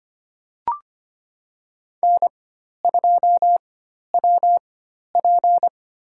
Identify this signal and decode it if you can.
Morse code, heard as a keyed tone, about 25 wpm, spaced as for 12 wpm, 700 Hz. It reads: N2WP
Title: USA Call Signs - 4 Characters - 4x Character Spacing - 25wpm